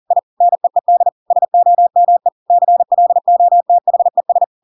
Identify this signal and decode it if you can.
Morse code, heard as a keyed tone, about 40 wpm, 700 Hz. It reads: I NEED SOME CLOTHES
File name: Pinocchio / Pinocchio ch8 40 40 ik1hge